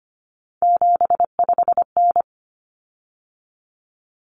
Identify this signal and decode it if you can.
Morse code, heard as a keyed tone, about 25 wpm, 700 Hz. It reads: 75N
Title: Common Cut Numbers - 3-Digit 25wpm